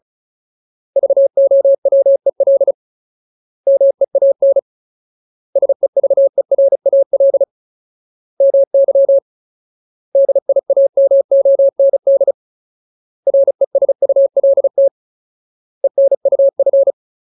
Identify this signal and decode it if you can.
Morse code, heard as a keyed tone, about 34 wpm, 550 Hz. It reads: VOWEL MEAN SEVERAL MY DIAMOND RESULT ENUF
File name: Ultimate Word List / 35WPM Ultimate